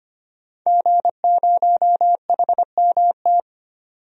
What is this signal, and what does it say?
Morse code, heard as a keyed tone, about 25 wpm, 700 Hz. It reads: G0HMT
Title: DXCC-100 Call Signs - 5 Characters - 25wpm